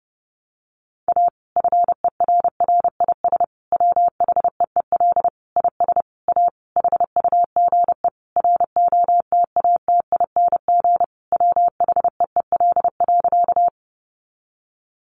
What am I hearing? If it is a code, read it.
Morse code, heard as a keyed tone, about 30 wpm, 700 Hz. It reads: A FERRIS WHEEL IS A HUGE ROTATING WHEEL.
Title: Additional Easy Phrases 30wpm